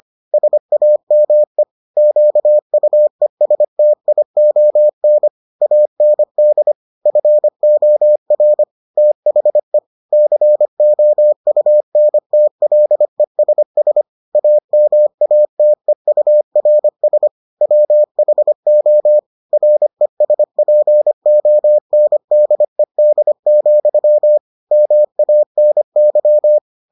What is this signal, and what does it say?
Morse code, heard as a keyed tone, about 25 wpm, 600 Hz. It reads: SAME QUESTION AND FOR THE COUNTLESS AMATEURS WHO RESPONDED, MANY